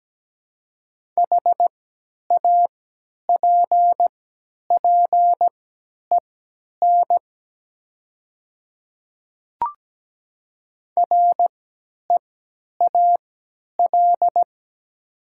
Morse code, heard as a keyed tone, about 17 wpm, spaced as for 10 wpm, 700 Hz. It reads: HAPPEN REAL
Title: 3x Character Spacing - Top 500 Words - 17wpm